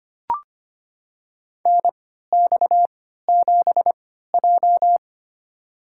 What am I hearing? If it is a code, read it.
Morse code, heard as a keyed tone, about 25 wpm, spaced as for 14 wpm, 700 Hz. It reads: NX7J
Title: USA Call Signs - 4 Characters - 3x Character Spacing - 25wpm